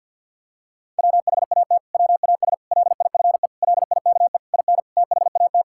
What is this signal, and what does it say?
Morse code, heard as a keyed tone, about 50 wpm, 700 Hz. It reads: WHAT WAS LIFE LIKE IN THAT
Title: Sentences from Top 1000 Words - Review Part 5 - 50wpm